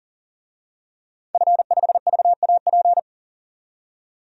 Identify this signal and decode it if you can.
Morse code, heard as a keyed tone, about 40 wpm, 700 Hz. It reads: F5VAP
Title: DXCC-100 Call Signs - 5 Characters - 40wpm